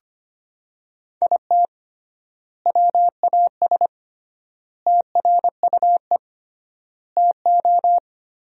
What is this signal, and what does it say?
Morse code, heard as a keyed tone, about 25 wpm, 700 Hz. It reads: IT WAS TRUE TO
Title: 3x Word-Spacing - Sentences from Top 400 Words - 25wpm